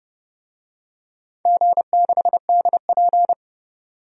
Morse code, heard as a keyed tone, about 30 wpm, 700 Hz. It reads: G6DP
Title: DXCC-100 Call Signs - 4 Characters - 30wpm